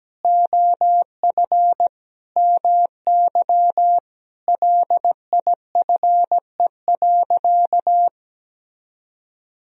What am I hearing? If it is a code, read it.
Morse code, heard as a keyed tone, about 17 wpm, 700 Hz. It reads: OF MY LIFE.
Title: Sentences from Top 1000 Words - Review Part 1 - 17wpm